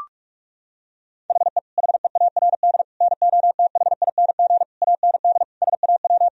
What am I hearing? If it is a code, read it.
Morse code, heard as a keyed tone, about 41 wpm, 700 Hz. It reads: HE HEARD NOTHING AND SAW
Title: Sentences from Top 1000 Words - Review Part 3 - 45wpm